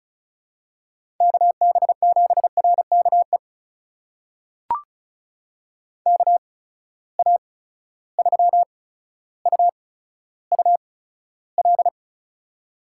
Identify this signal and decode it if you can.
Morse code, heard as a keyed tone, about 36 wpm, spaced as for 15 wpm, 700 Hz. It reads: KB7RKE KA3UUL
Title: USA Call Signs - 6 Characters - 8x Character Spacing - 35wpm